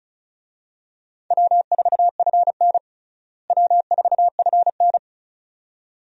Morse code, heard as a keyed tone, about 36 wpm, 700 Hz. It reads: W4FN W4FN
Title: USA Call Signs - 4 Characters - Repeated 2x - 35wpm